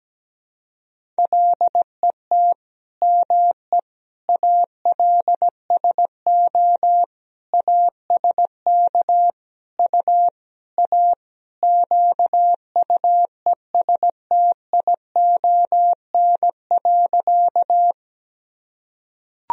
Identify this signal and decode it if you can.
Morse code, heard as a keyed tone, about 17 wpm, 700 Hz. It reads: LET ME ALSO ASK U A QUESTION.
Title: Sentences from Top 400 Words and Abbreviations 17wpm